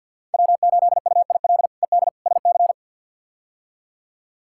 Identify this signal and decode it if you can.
Morse code, heard as a keyed tone, about 51 wpm, 700 Hz. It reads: W8UIL ED SC
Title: K1USN SST QSO Practice - 50wpm